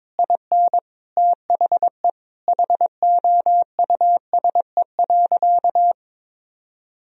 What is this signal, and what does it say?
Morse code, heard as a keyed tone, about 22 wpm, 700 Hz. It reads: IN THE HOUSE.
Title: Sentences from Top 1000 Words - Review Part 4 - 22wpm